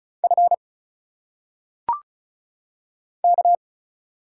Morse code, heard as a keyed tone, about 35 wpm, 700 Hz. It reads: F K
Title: Single Letters - Phonetic Pronunciation 35wpm